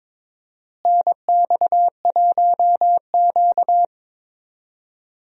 Morse code, heard as a keyed tone, about 23 wpm, 700 Hz. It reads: NX1Q
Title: USA Call Signs - 4 Characters - Repeated 3x - 22wpm